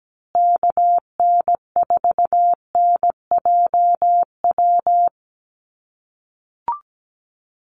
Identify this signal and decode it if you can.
Morse code, heard as a keyed tone, about 17 wpm, 700 Hz. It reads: KN4NJW
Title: USA Call Signs Repeated 2x 17wpm